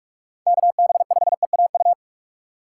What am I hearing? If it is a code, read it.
Morse code, heard as a keyed tone, about 44 wpm, 700 Hz. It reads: KB5EAU
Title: USA Call Signs - 6 Characters - 4x Character Spacing - 45wpm